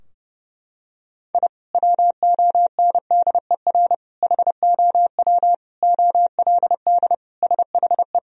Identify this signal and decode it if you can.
Morse code, heard as a keyed tone, about 29 wpm, 700 Hz. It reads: I WONDER HOW OLD SHE
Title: Sentences from Top 1000 Words - Review Part 3 - 30wpm